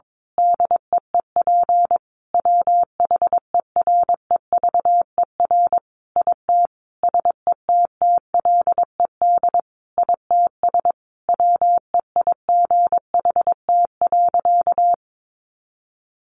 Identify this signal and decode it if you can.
Morse code, heard as a keyed tone, about 22 wpm, 700 Hz. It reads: DEEP WHEREVER IT SETTLED ITS WEIGHT.
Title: A Sound Of Thunder 22wpm